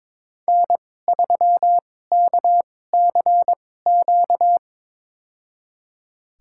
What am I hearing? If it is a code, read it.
Morse code, heard as a keyed tone, about 22 wpm, spaced as for 16 wpm, 700 Hz. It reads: N3KCQ
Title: USA Call Signs - 5 Characters - 2x Character Spacing - 22wpm